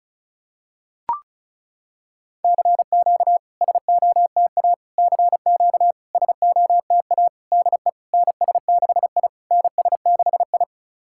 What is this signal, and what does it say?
Morse code, heard as a keyed tone, about 36 wpm, 690 Hz. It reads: CQ SOTA CQ SOTA DE NS6I NS6I
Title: SOTA QSO Practice 35wpm